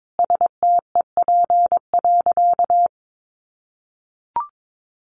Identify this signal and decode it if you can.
Morse code, heard as a keyed tone, about 22 wpm, 700 Hz. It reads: STEP.
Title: Sentences from Top 1400 Words 22wpm